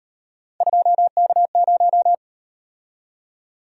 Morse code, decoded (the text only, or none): 2K0